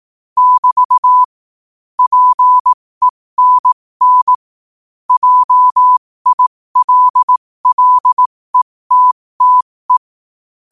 Morse code, heard as a keyed tone, about 18 wpm, spaced as for 15 wpm, 1000 Hz. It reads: <BT> PENN JILLETTE
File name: qotdfiles / 15wpm